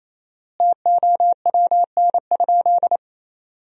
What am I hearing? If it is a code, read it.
Morse code, heard as a keyed tone, about 28 wpm, 700 Hz. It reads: TOWN?
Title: 4x Word-Spacing - Sentences from Top 500 Words - 28wpm